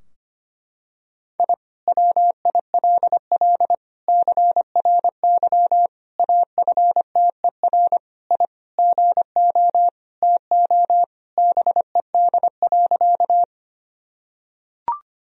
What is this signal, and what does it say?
Morse code, heard as a keyed tone, about 25 wpm, 700 Hz. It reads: I WILL CRY AFTER I GO TO BED.
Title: Sentences from Top 1000 Words - Review Part 2 - 25wpm